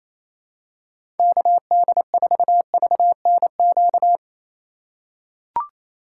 Morse code, heard as a keyed tone, about 28 wpm, 700 Hz. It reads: KD4VNQ
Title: USA Call Signs - 3x Character Spacing - 28wpm